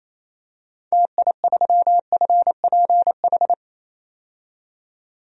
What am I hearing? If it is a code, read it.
Morse code, heard as a keyed tone, about 27 wpm, 700 Hz. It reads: TI3FPH
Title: DXCC-100 Call Signs - 6 Characters - 28wpm